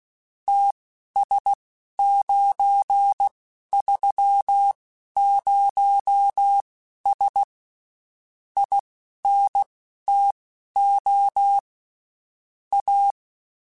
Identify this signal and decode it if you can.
Morse code, heard as a keyed tone, about 16 wpm, spaced as for 11 wpm, 800 Hz. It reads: TS930S INTO A
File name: (spaced) / QSOs Spaced-mp3s / kc7haa @15 sp